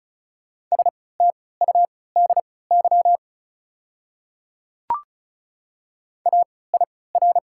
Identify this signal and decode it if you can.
Morse code, heard as a keyed tone, about 36 wpm, spaced as for 20 wpm, 700 Hz. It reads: STUDY AIR
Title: 3x Character Spacing - Top 300 Words - 35wpm